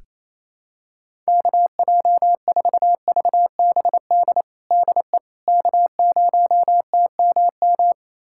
Morse code, heard as a keyed tone, about 27 wpm, 700 Hz. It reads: KJ4VBD DE K0TMM
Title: USA Callsign DE Callsign 28wpm